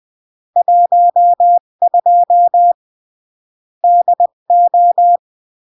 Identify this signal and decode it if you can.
Morse code, heard as a keyed tone, about 20 wpm, 700 Hz. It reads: 12 DO
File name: CW / jodorowsky / 82 commandments 20-15